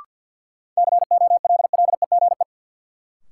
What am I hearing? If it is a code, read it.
Morse code, heard as a keyed tone, about 51 wpm, 700 Hz. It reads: COLLEGE
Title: Sentences from Top 1300 Words - New Word Drill Alt - 50wpm